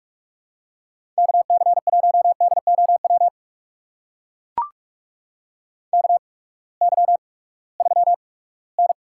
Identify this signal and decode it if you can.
Morse code, heard as a keyed tone, about 45 wpm, spaced as for 22 wpm, 700 Hz. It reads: KC1DOW KY3N
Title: USA Call Signs - 8x Character Spacing - 45wpm